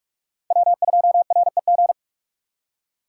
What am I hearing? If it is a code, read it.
Morse code, heard as a keyed tone, about 42 wpm, 700 Hz. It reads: W2REG
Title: USA Call Signs - 5 Characters - 2x Character Spacing - 45wpm